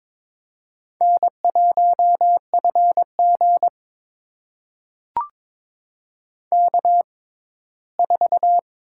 Morse code, heard as a keyed tone, about 22 wpm, spaced as for 15 wpm, 700 Hz. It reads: N1FG K4
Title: USA Call Signs - 6x Character Spacing - 22wpm